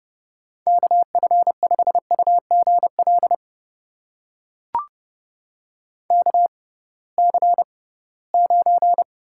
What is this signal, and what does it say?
Morse code, heard as a keyed tone, about 30 wpm, spaced as for 19 wpm, 700 Hz. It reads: KF5UGL KC9
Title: USA Call Signs - 6 Characters - 6x Character Spacing - 30wpm